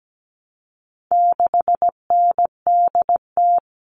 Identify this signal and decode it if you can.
Morse code, heard as a keyed tone, about 17 wpm, 700 Hz. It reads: 6NDT
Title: All Cut Numbers - 4-Digit 17wpm